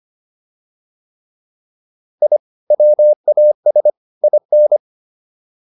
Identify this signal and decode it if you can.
Morse code, heard as a keyed tone, about 25 wpm, 600 Hz. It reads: I WAS IN
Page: intermediate-practice-files